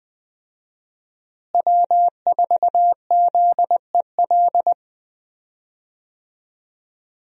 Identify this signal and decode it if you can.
Morse code, heard as a keyed tone, about 20 wpm, 700 Hz. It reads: W4ZEL
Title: USA Call Signs - 5 Characters - Repeated 2x - 20wpm